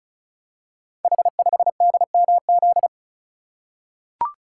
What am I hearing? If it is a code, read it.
Morse code, heard as a keyed tone, about 36 wpm, 700 Hz. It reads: H5DMZ
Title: Code Groups - 5 Letter:Numbers - 35wpm